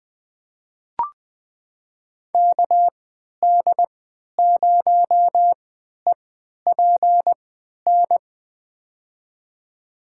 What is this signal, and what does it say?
Morse code, heard as a keyed tone, about 20 wpm, spaced as for 11 wpm, 700 Hz. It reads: KD0EPN
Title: USA Call Signs - 6 Characters - 3x Character Spacing - 20wpm